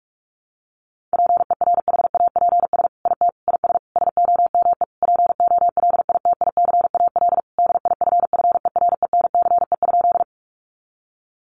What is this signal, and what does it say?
Morse code, heard as a keyed tone, about 44 wpm, 700 Hz. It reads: PERHAPS IT IS SOME POLITICAL DIFFERENCE?
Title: Sentences from Top 800 Words 45wpm